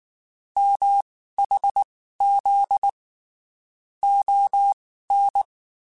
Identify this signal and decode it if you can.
Morse code, heard as a keyed tone, about 19 wpm, spaced as for 14 wpm, 800 Hz. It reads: MHZ ON
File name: (spaced) / QSOs Spaced-mp3s / WA7QOX 2 @18 sp